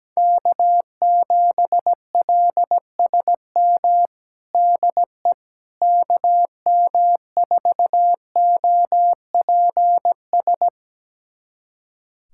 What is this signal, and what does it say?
Morse code, heard as a keyed tone, about 17 wpm, 700 Hz. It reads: K7LSM DE KM4OPS
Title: USA Callsign DE Callsign 17wpm